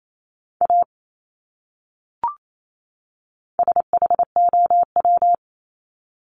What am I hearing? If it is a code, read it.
Morse code, heard as a keyed tone, about 28 wpm, 700 Hz. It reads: A SHOW
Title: Top 300 Words 28wpm